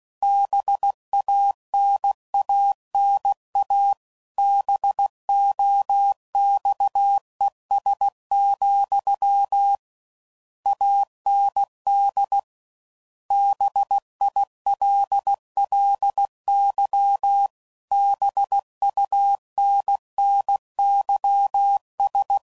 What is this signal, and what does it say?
Morse code, heard as a keyed tone, about 16 wpm, 800 Hz. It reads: BANANA BOXES, AND BILLY BUNNYS